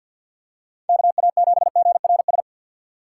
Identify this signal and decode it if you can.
Morse code, heard as a keyed tone, about 47 wpm, 700 Hz. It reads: KA7GRS
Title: USA Call Signs - 6x Character Spacing - 50wpm